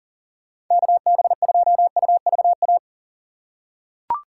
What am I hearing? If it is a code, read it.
Morse code, heard as a keyed tone, about 40 wpm, 700 Hz. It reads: KB2UVA
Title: USA Call Signs - 6 Characters - 2x Character Spacing - 40wpm